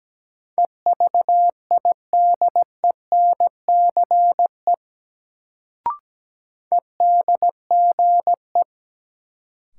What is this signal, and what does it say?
Morse code, heard as a keyed tone, about 17 wpm, 700 Hz. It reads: EVIDENCE EDGE
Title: Sentences from Top 1300 Words - New Word Drill - 17wpm